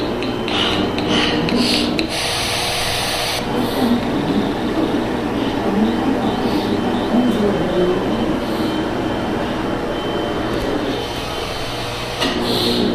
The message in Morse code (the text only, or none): none